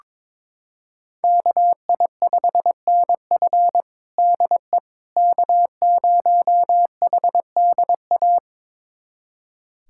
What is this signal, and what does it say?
Morse code, heard as a keyed tone, about 22 wpm, 700 Hz. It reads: KI5NF DE K0HDA
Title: USA Callsign DE Callsign - 5 Characters - 22wpm